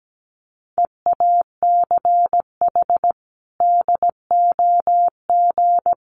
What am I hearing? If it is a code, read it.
Morse code, heard as a keyed tone, about 17 wpm, 700 Hz. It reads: EACH DOG
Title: Sentences from Top 1300 Words 17wpm